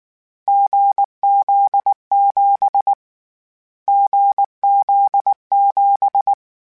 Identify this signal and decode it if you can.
Morse code, heard as a keyed tone, about 19 wpm, 800 Hz. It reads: GZ7 GZ7